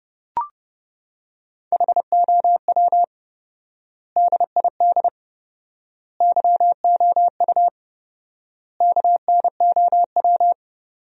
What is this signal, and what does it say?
Morse code, heard as a keyed tone, about 30 wpm, 700 Hz. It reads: HOW DID YOU KNOW
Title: 4x Word-Spacing - Sentences from Top 100 Words - 30wpm